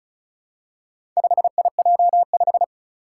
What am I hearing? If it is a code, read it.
Morse code, heard as a keyed tone, about 36 wpm, 700 Hz. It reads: 5IJ5
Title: Four Challenging Letter-Number Combinations 35wpm